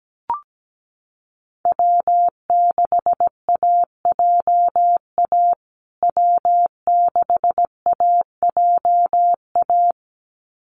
Morse code, heard as a keyed tone, about 17 wpm, 700 Hz. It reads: W6AJA W6AJA
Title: USA Call Signs Repeated 2x 17wpm